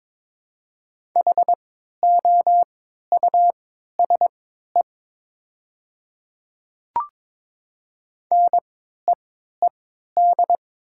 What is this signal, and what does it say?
Morse code, heard as a keyed tone, about 22 wpm, spaced as for 13 wpm, 700 Hz. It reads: HOUSE NEED
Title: 3x Character Spacing - Top 500 Words - 22wpm